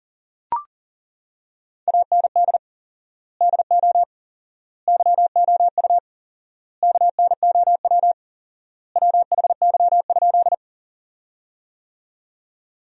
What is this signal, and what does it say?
Morse code, heard as a keyed tone, about 40 wpm, 700 Hz. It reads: AND DO YOU KNOW WHY?
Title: 4x Word-Spacing - Sentences from Top 200 Words - 40wpm